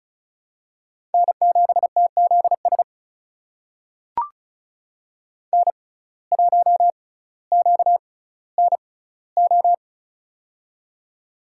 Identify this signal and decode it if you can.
Morse code, heard as a keyed tone, about 36 wpm, spaced as for 18 wpm, 700 Hz. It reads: N7TZS N1QNO
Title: USA Call Signs - 5 Characters - 6x Character Spacing - 35wpm